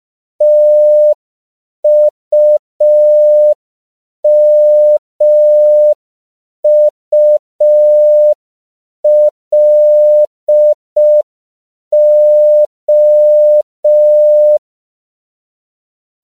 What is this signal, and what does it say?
Morse code, heard as a keyed tone, about 5 wpm, 600 Hz. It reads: TUMULO
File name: CW / 5PPM